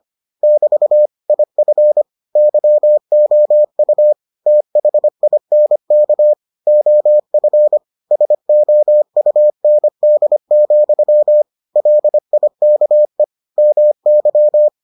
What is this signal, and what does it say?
Morse code, heard as a keyed tone, about 25 wpm, 600 Hz. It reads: <BT> IF YOU THINK OF SOUND, LIKE MY